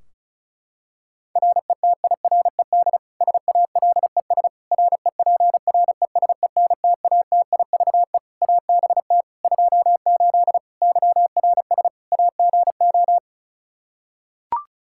Morse code, heard as a keyed tone, about 34 wpm, 700 Hz. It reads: RETIRED SALES REPRESENTATIVE ABT 28 YRS AGO
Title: Practice QSO - Exchange 3 - More Info - 35wpm